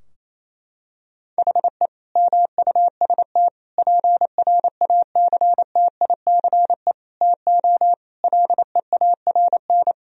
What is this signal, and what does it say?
Morse code, heard as a keyed tone, about 28 wpm, 700 Hz. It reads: HE MUST PRACTICE TO LEARN
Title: Sentences from Top 1000 Words - Review Part 4 - 28wpm